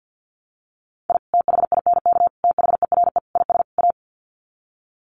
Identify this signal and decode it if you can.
Morse code, heard as a keyed tone, about 50 wpm, 700 Hz. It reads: I THINK THERE IS A
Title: Sets of 5 Words 50wpm